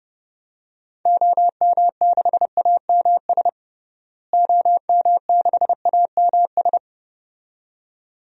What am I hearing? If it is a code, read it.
Morse code, heard as a keyed tone, about 30 wpm, 700 Hz. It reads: OM6AMS OM6AMS
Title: DXCC-100 Call Signs - 6 Characters - Repeated 2x - 30wpm